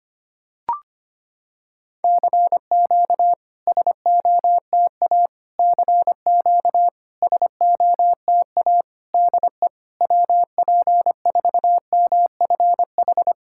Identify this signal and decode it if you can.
Morse code, heard as a keyed tone, about 25 wpm, 700 Hz. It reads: CQ SOTA CQ SOTA DE WP4MFH
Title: SOTA QSO Practice 25wpm